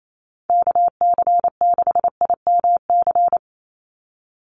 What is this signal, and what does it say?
Morse code, heard as a keyed tone, about 28 wpm, 700 Hz. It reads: KC6IMC
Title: USA Call Signs 28wpm